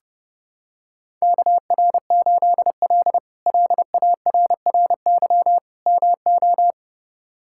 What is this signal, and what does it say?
Morse code, heard as a keyed tone, about 30 wpm, 700 Hz. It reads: KR8L LARRY MO
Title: K1USN SST QSO Practice - 30wpm